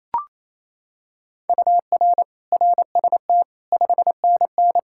Morse code, heard as a keyed tone, about 28 wpm, 700 Hz. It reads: UR RST 5NN
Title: K1USN SST QSO Practice 28wpm